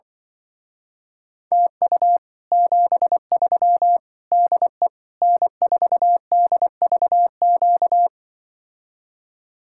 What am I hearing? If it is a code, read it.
Morse code, heard as a keyed tone, about 24 wpm, 700 Hz. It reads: TU 73 DE N4DVQ
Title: Speed-Racing Common QSO Phrases Encore - 20wpm